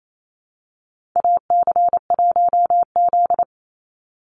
Morse code, heard as a keyed tone, about 28 wpm, 700 Hz. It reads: AC1Z
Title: NA SOTA Chaser Callsigns 28wpm